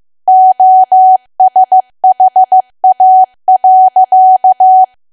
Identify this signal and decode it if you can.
Morse code, heard as a keyed tone, about 15 wpm, 750 Hz. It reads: OSHA.
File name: cwa / W1AW-Level2 / 15WPM